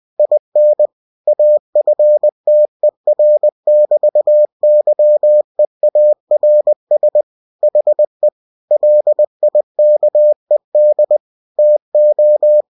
Morse code, heard as a keyed tone, about 20 wpm, 600 Hz. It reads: IN AFTER<BT>YEARS HE LIKED TO
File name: cw-books / winnie-the-pooh / winnie-20wpm-page-0100